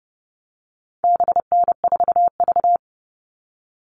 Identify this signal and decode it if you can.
Morse code, heard as a keyed tone, about 30 wpm, 700 Hz. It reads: BN4V
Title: All Cut Numbers - 4-Digit 30wpm